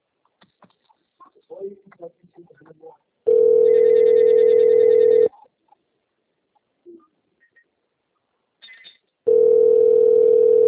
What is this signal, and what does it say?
Ringback tone (North America), 440 + 480 Hz, with speech.
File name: rg-1600-unknown- - - .32410